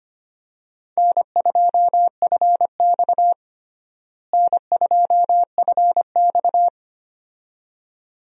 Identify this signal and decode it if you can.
Morse code, heard as a keyed tone, about 25 wpm, 700 Hz. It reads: N2FX N2FX
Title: USA Call Signs - 4 Characters - Repeated 2x - 25wpm